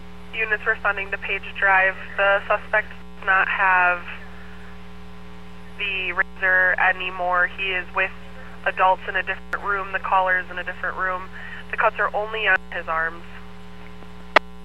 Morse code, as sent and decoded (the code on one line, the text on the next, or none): none